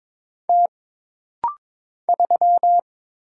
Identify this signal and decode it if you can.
Morse code, heard as a keyed tone, about 22 wpm, 700 Hz. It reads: T 3
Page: This rapid-fire timing is geared to push your ICR (Instant Character Recognition) skills.